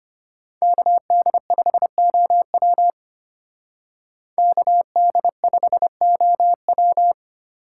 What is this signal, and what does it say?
Morse code, heard as a keyed tone, about 30 wpm, 700 Hz. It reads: KD5OW KD5OW
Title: Speed Racing - USA 2x2 Call Signs 25wpm